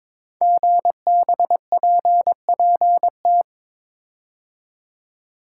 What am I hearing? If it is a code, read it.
Morse code, heard as a keyed tone, about 22 wpm, 700 Hz. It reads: GBPPT
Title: Code Groups - 5 Letters - 22wpm